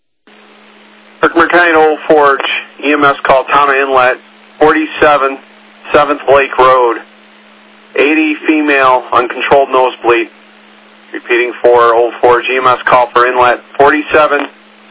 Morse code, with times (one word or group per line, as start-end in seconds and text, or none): none